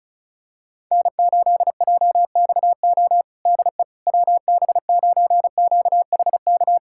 Morse code, heard as a keyed tone, about 36 wpm, 700 Hz. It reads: N8JXO DE WB9QHK
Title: USA Callsign DE Callsign 35wpm